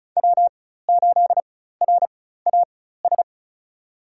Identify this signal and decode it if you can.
Morse code, heard as a keyed tone, about 36 wpm, spaced as for 16 wpm, 700 Hz. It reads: W8RAS
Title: USA Call Signs - 5 Characters - 4x Character Spacing - 35wpm